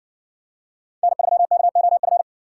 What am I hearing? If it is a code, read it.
Morse code, heard as a keyed tone, about 53 wpm, 700 Hz. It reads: N3KOF